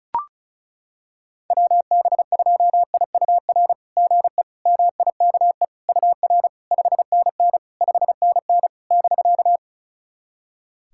Morse code, heard as a keyed tone, about 36 wpm, 690 Hz. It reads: WB2IUR GE MIKE UR 5NN 5NN <BK>
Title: SOTA QSO Practice 35wpm